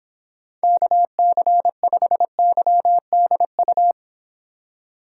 Morse code, heard as a keyed tone, about 26 wpm, 700 Hz. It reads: KC5YDU